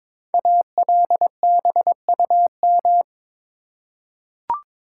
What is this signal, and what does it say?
Morse code, heard as a keyed tone, about 23 wpm, 700 Hz. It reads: ALBUM
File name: 5 Letter Words - First 3 Letters Given - 22wpm